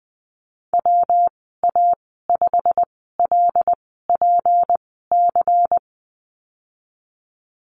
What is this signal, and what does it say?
Morse code, heard as a keyed tone, about 20 wpm, spaced as for 15 wpm, 700 Hz. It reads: WA5LPC
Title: 173 Lesson 28g - Number 9 - Call Signs 20wpm